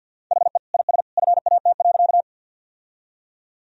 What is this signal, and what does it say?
Morse code, heard as a keyed tone, about 50 wpm, 700 Hz. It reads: HE IS FAT.
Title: Sentences from Top 1000 Words - Review Part 4 - 50wpm